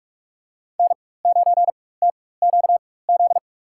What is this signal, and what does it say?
Morse code, heard as a keyed tone, about 45 wpm, spaced as for 21 wpm, 700 Hz. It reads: N9TQZ
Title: USA Call Signs - 5 Characters - 4x Character Spacing - 45wpm